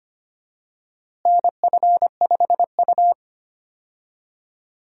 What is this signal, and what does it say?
Morse code, heard as a keyed tone, about 25 wpm, 700 Hz. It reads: NF5U
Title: USA Call Signs - 4 Characters - Repeated 3x - 25wpm